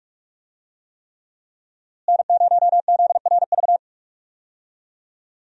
Morse code, heard as a keyed tone, about 46 wpm, 700 Hz. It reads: N0ZRV
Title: USA Call Signs - 5 Characters - Repeated 3x - 45wpm